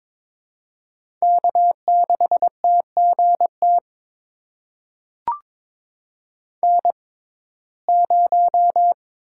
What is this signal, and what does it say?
Morse code, heard as a keyed tone, about 21 wpm, spaced as for 16 wpm, 700 Hz. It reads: K6TGT N0
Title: USA Call Signs - 5 Characters - 6x Character Spacing - 22wpm